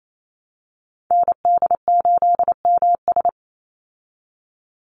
Morse code, heard as a keyed tone, about 28 wpm, 700 Hz. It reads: ND8MS
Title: Call Signs Repeated 2x - 28wpm